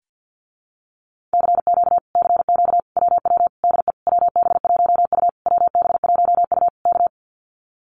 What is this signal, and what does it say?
Morse code, heard as a keyed tone, about 50 wpm, 700 Hz. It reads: CQ CQ WW DE WB1U WB1U K
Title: Common QSO Phrases Encore 50wpm